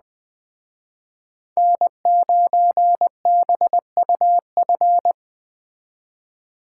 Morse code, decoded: N9BUF

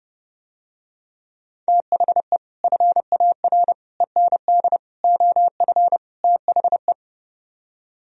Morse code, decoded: THE FAR END OF THE

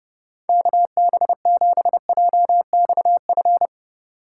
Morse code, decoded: KB7JXF

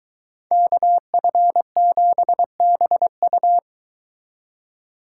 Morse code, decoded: KF7BU

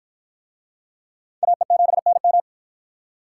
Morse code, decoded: AE7NM

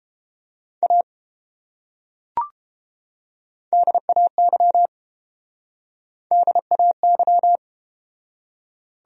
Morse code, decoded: A DAY DAY

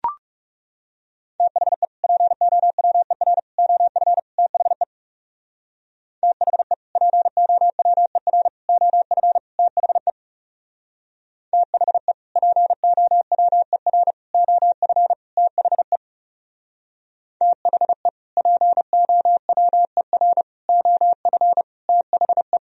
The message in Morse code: THE POWER OF THE THE POWER OF THE THE POWER OF THE THE POWER OF THE

